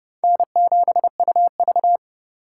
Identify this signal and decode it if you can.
Morse code, heard as a keyed tone, about 35 wpm, 700 Hz. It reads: N7UV